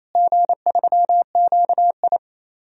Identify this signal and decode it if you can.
Morse code, heard as a keyed tone, about 28 wpm, 700 Hz. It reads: G3QI
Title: DXCC-100 Call Signs - 4 Characters - Repeated 3x - 28wpm